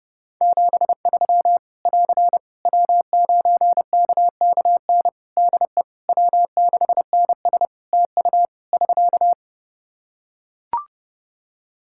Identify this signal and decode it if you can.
Morse code, heard as a keyed tone, about 30 wpm, 700 Hz. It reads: 73 <AR> W9KKN DE W6NS TU <SK>
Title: Practice QSO - Exchange 4 - Ending - 30wpm